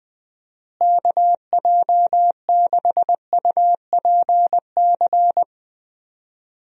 Morse code, heard as a keyed tone, about 21 wpm, 700 Hz. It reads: KJ6UPC
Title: USA Call Signs - 6 Characters - Repeated 2x - 20wpm